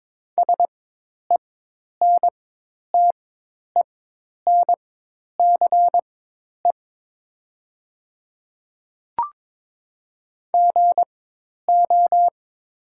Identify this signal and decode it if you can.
Morse code, heard as a keyed tone, about 21 wpm, spaced as for 10 wpm, 700 Hz. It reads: SENTENCE GO
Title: 4x Character Spacing - Top 200 Words - 22wpm